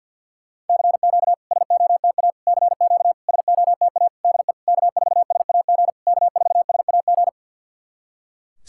Morse code, heard as a keyed tone, about 50 wpm, 700 Hz. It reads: CQ SOTA CQ SOTA DE K4SAG K4SAG
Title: SOTA QSO Practice 50wpm